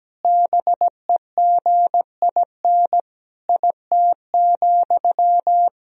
Morse code, heard as a keyed tone, about 17 wpm, 700 Hz. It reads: BEGIN IT,
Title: Sentences from Top 1000 Words - Review Part 1 - 17wpm